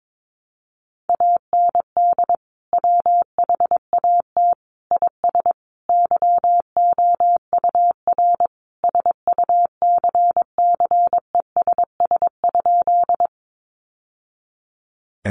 Morse code, heard as a keyed tone, about 22 wpm, 700 Hz. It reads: AND WHAT IS YOUR SUCCESS?
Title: Sentences from Top 900 Words 22wpm